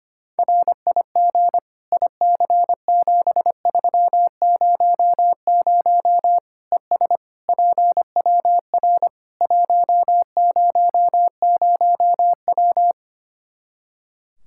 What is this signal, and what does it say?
Morse code, heard as a keyed tone, about 25 wpm, 700 Hz. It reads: RIG IC7300 ES PWR 100W
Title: Practice QSO - Exchange 2 - Info - 25wpm